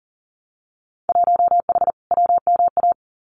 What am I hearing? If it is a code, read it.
Morse code, heard as a keyed tone, about 41 wpm, 700 Hz. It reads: 1H WMA
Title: Winter Field Day Exchanges 40wpm